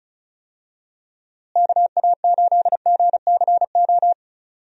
Morse code, heard as a keyed tone, about 36 wpm, 700 Hz. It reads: KA8GCO
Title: USA Call Signs - 6 Characters - Repeated 2x - 35wpm